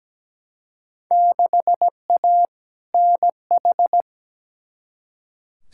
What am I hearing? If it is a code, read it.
Morse code, heard as a keyed tone, about 17 wpm, 700 Hz. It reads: 6A NH
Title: Field Day Exchanges - 17wpm